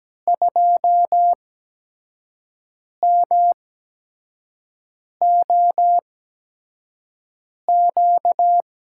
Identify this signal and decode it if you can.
Morse code, heard as a keyed tone, about 17 wpm, spaced as for 5 wpm, 700 Hz. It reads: 2MOQ
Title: USA Call Signs - 5 Characters - 8x Character Spacing - 17wpm